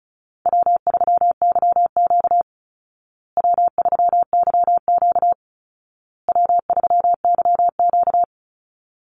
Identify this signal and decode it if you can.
Morse code, heard as a keyed tone, about 34 wpm, 700 Hz. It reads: W3YQ W3YQ W3YQ
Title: Easy Call Signs Repeated 3x - 35wpm